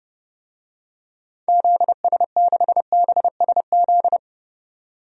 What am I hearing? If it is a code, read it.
Morse code, heard as a keyed tone, about 30 wpm, 700 Hz. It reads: ZS6BSZ
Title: DXCC-100 Call Signs - 6 Characters - 30wpm